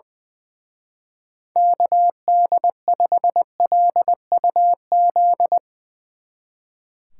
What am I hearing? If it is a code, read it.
Morse code, heard as a keyed tone, about 21 wpm, 700 Hz. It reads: KD5LUZ
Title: Speed Racing - USA 2x3 Call Signs 20wpm